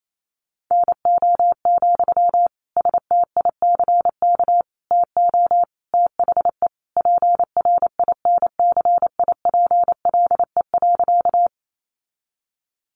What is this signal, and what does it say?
Morse code, heard as a keyed tone, about 28 wpm, 700 Hz. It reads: NO, STICK TO THE PRINCIPLE.
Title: Sentences from Top 1400 Words 28wpm